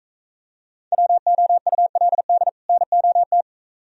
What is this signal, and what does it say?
Morse code, heard as a keyed tone, about 42 wpm, 700 Hz. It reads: WOULD NOT